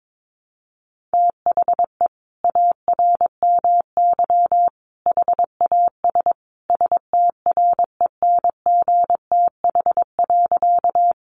Morse code, heard as a keyed tone, about 22 wpm, 700 Hz. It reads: THE ARMY HAS STRENGTH.
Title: Sentences from Top 1300 Words 22wpm